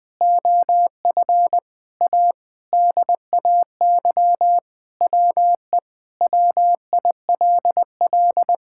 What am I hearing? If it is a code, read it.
Morse code, heard as a keyed tone, about 20 wpm, 700 Hz. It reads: OF A DAY WE WILL
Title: Sentences with 3-Word Phrases - 20wpm